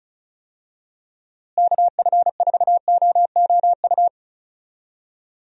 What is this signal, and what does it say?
Morse code, heard as a keyed tone, about 34 wpm, 700 Hz. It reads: KF4OOU
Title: USA Call Signs - 6 Characters - Repeated 2x - 35wpm